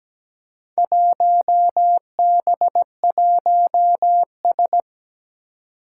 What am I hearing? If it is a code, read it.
Morse code, heard as a keyed tone, about 17 wpm, 700 Hz. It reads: 1B1S
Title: Four Challenging Letter-Number Combinations 17wpm